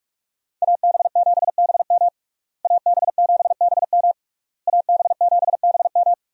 Morse code, heard as a keyed tone, about 41 wpm, 700 Hz. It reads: AB7BM AB7BM AB7BM
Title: USA Call Signs - 5 Characters - Repeated 3x - 45wpm